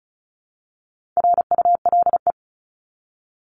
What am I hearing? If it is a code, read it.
Morse code, heard as a keyed tone, about 34 wpm, 700 Hz. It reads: RULE
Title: ES:S Suffix - Top 100 Words with Base Words 35wpm